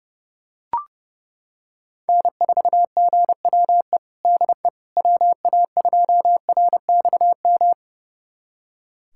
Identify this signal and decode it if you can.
Morse code, heard as a keyed tone, about 30 wpm, 700 Hz. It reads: N4GWE DE WA2RXM
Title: USA Callsign DE Callsign 30wpm